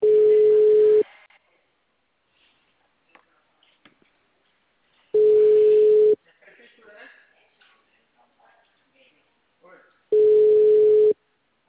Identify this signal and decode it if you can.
Ringback tone (Europe, 425 Hz), with speech.
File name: out- -2037- - - .75719